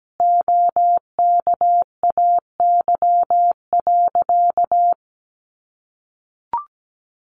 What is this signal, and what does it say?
Morse code, heard as a keyed tone, about 17 wpm, 700 Hz. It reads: OKAY.
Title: A Sound Of Thunder 17wpm